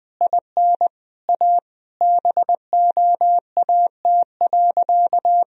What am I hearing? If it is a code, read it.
Morse code, heard as a keyed tone, about 20 wpm, 700 Hz. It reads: IN A BOAT.
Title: Sentences from Top 1000 Words - Review Part 3 - 20wpm